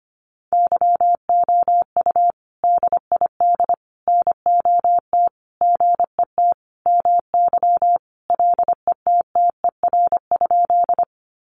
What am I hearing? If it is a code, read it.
Morse code, heard as a keyed tone, about 25 wpm, 700 Hz. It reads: YOU DID NOT GET MY LETTER?
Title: Sentences from Top 600 Words 25wpm